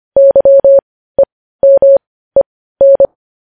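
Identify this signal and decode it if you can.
Morse code, heard as a keyed tone, about 25 wpm, spaced as for 15 wpm, 550 Hz. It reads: YEMEN
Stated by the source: Morse code news headlines.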